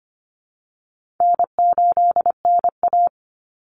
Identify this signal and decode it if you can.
Morse code, heard as a keyed tone, about 25 wpm, 700 Hz. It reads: N8NA
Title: Easy Call Signs Repeated 2x - 25wpm